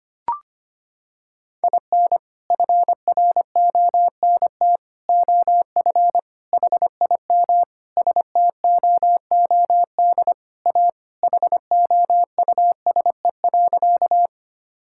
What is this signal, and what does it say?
Morse code, heard as a keyed tone, about 25 wpm, 700 Hz. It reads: IN FRONT OF HIM STOOD A HOUSE.
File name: practice / 25wpm / Sentences with 4-Word Phrases - 25wpm